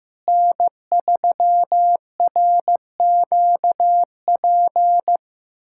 Morse code, heard as a keyed tone, about 15 wpm, 700 Hz. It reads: N3RQP
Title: USA Callsign DE Callsign - 5 Characters - 15wpm